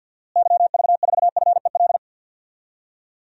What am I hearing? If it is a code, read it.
Morse code, heard as a keyed tone, about 49 wpm, 700 Hz. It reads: YV4FEL
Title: DXCC-100 Call Signs - 6 Characters - Repeated 3x - 50wpm